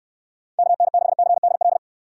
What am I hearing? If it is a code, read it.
Morse code, heard as a keyed tone, about 67 wpm, 700 Hz. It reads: KN6CDL